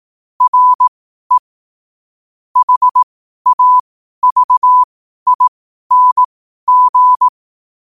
Morse code, heard as a keyed tone, about 18 wpm, spaced as for 12 wpm, 1000 Hz. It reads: RE HAVING
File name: 12wpm